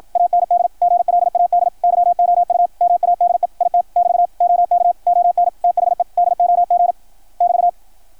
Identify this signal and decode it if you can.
Morse code, heard as a keyed tone, about 53 wpm, 660 Hz, with radio noise. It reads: AND GLAD YOU MADE IT <BT> OK ON THE DOG <BT>
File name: archived / CRSnet